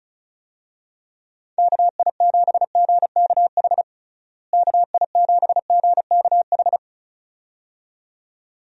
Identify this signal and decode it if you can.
Morse code, heard as a keyed tone, about 36 wpm, 700 Hz. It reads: KI7GKH KI7GKH
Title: USA Call Signs - 6 Characters - Repeated 2x - 35wpm